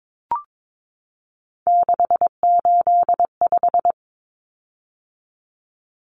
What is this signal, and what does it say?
Morse code, heard as a keyed tone, about 22 wpm, 700 Hz. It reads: 685
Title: Common Cut Numbers - 3-Digit 22wpm